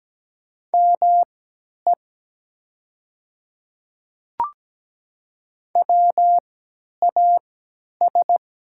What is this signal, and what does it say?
Morse code, heard as a keyed tone, about 17 wpm, spaced as for 10 wpm, 700 Hz. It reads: ME WAS
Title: 3x Character Spacing - Top 200 Words - 17wpm